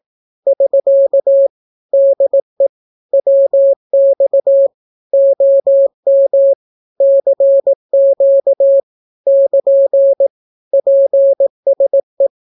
Morse code, heard as a keyed tone, about 18 wpm, 550 Hz. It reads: <SK> DE WX OM CQ <KN> PSE
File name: naukapodstaw / skroty / skroty